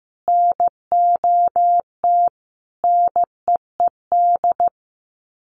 Morse code, heard as a keyed tone, about 15 wpm, 700 Hz. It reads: NOT NEED
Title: Sets of 4 Words 15wpm